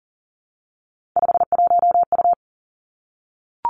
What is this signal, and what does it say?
Morse code, heard as a keyed tone, about 42 wpm, 690 Hz. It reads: 51U
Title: Three Challenging Letter-Number Combinations 40wpm